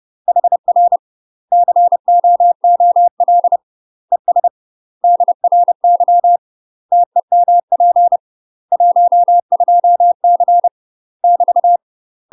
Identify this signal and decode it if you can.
Morse code, heard as a keyed tone, about 30 wpm, 700 Hz. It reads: HR COOL ES DRY TEMP 12C <BT>
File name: UK MASTER QSOs TRAINING / 30WPM / UK MASTER QSOs 0036